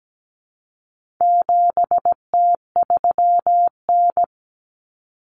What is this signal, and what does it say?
Morse code, heard as a keyed tone, about 17 wpm, 700 Hz. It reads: 7T3N
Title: Common Cut Numbers - 4-Digit 17wpm